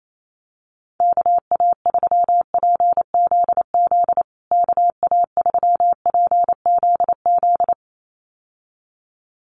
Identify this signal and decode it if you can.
Morse code, heard as a keyed tone, about 28 wpm, 700 Hz. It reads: KA3PZZ KA3PZZ
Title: USA Call Signs Repeated 2x 28wpm